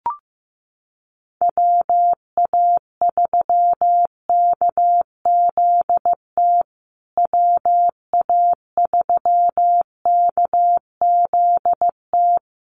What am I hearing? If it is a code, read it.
Morse code, heard as a keyed tone, about 15 wpm, 700 Hz. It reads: WA3KZT WA3KZT
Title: USA Call Signs Repeated 3x 15wpm